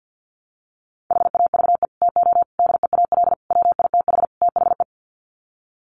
Morse code, heard as a keyed tone, about 50 wpm, 700 Hz. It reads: HAVE TO DEAL WITH THE
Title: Sets of 5 Words Encore2 50wpm